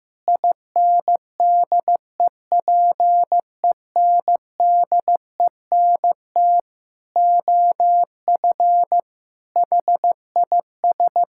Morse code, heard as a keyed tone, about 15 wpm, 700 Hz. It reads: INDEPENDENT OF HIS
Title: Sentences from Top 1000 Words - Review Part 6 - 15wpm